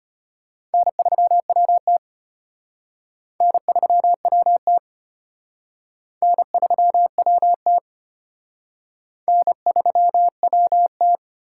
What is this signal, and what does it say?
Morse code, heard as a keyed tone, about 33 wpm, 700 Hz. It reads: N3WT N3WT N3WT N3WT